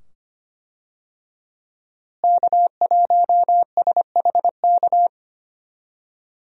Morse code, heard as a keyed tone, about 25 wpm, 700 Hz. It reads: K1SHK
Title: USA Call Signs - 5 Characters - Repeated 2x - 25wpm